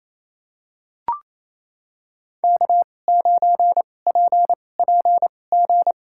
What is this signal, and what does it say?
Morse code, heard as a keyed tone, about 28 wpm, spaced as for 20 wpm, 700 Hz. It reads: K9PPG
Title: USA Call Signs - 5 Characters - 2x Character Spacing - 28wpm